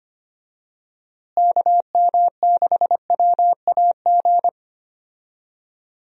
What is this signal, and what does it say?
Morse code, heard as a keyed tone, about 25 wpm, 700 Hz. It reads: KM6WAG
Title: USA Call Signs - 6 Characters - Repeated 3x - 25wpm